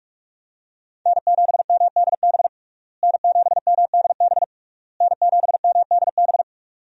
Morse code, heard as a keyed tone, about 46 wpm, 700 Hz. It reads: N7MDB N7MDB N7MDB
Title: USA Call Signs - 5 Characters - Repeated 3x - 45wpm